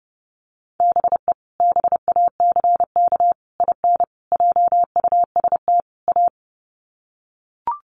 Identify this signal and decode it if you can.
Morse code, heard as a keyed tone, about 30 wpm, 700 Hz. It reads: BE BACK IN JUST A
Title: Sets of 5 Words Encore 30wpm